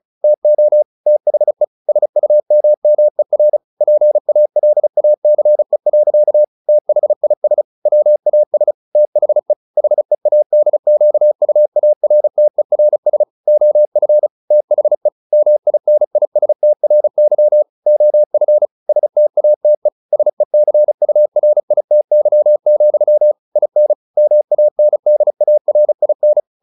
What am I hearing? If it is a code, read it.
Morse code, heard as a keyed tone, about 34 wpm, 600 Hz. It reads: TO THE SUMMER PALACE. THIS WAS THE HEADQUARTERS OF THE MINISTRY OF STATE SECURITY, IN MANDARIN